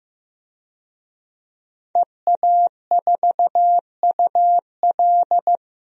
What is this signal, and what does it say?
Morse code, heard as a keyed tone, about 15 wpm, 700 Hz. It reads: EA4UL
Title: DXCC-100 Call Signs - 5 Characters - 15wpm